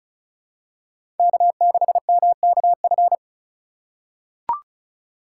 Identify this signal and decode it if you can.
Morse code, heard as a keyed tone, about 36 wpm, 700 Hz. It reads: K6MKF
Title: K1USN SST QSO Practice - 35wpm